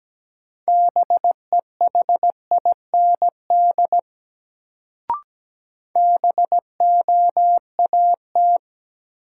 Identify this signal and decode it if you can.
Morse code, heard as a keyed tone, about 17 wpm, 700 Hz. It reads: BEHIND BOAT
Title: Sentences from Top 500 Words - New Word Drill - 17wpm